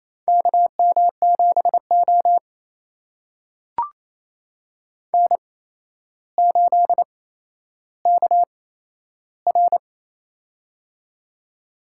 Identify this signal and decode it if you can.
Morse code, heard as a keyed tone, about 27 wpm, spaced as for 12 wpm, 700 Hz. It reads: KM7O N8KR
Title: USA Call Signs - 4 Characters - 8x Character Spacing - 28wpm